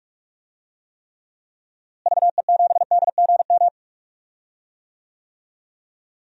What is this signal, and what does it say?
Morse code, heard as a keyed tone, about 44 wpm, 700 Hz. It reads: VE7DGM
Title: DXCC-100 Call Signs - 6 Characters - Repeated 2x - 45wpm